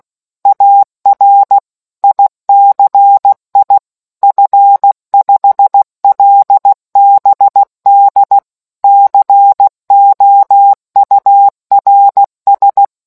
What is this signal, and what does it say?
Morse code, heard as a keyed tone, about 16 wpm, 800 Hz. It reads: AR ICI F5LBD COURS